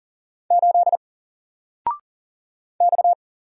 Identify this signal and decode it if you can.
Morse code, heard as a keyed tone, about 38 wpm, 700 Hz. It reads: 8 X
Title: Single Letter-Number Alt 40wpm